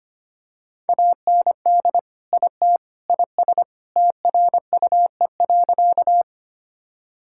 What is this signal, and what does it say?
Morse code, heard as a keyed tone, about 25 wpm, 700 Hz. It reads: AND IT IS TRUE.
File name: practice / 25wpm / Sentences with 3-Word Phrases - 25wpm